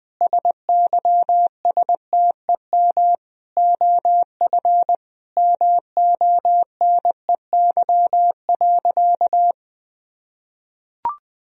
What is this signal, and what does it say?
Morse code, heard as a keyed tone, about 20 wpm, 700 Hz. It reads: SYSTEM OF MONEY.
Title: Sentences from Top 1000 Words - Review Part 4 - 20wpm